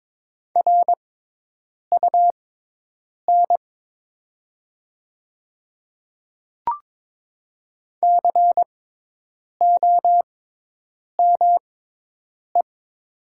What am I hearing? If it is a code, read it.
Morse code, heard as a keyed tone, about 21 wpm, spaced as for 8 wpm, 700 Hz. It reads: RUN COME
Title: 6x Character Spacing - Top 300 Words - 22wpm